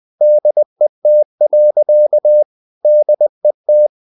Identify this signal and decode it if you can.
Morse code, heard as a keyed tone, about 20 wpm, 600 Hz. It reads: DET. DET
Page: Ferie 20 wpm